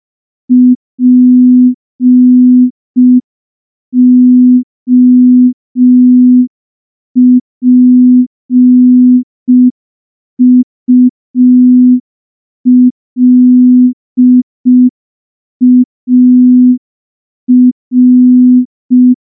Morse code, heard as a keyed tone, about 5 wpm, 250 Hz. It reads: POPULAR
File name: morse / morse-training-05WPM